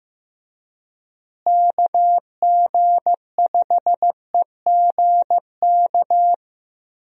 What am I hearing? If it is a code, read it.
Morse code, heard as a keyed tone, about 16 wpm, 700 Hz. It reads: KG5EGK
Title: USA Call Signs - 6 Characters - Repeated 2x - 15wpm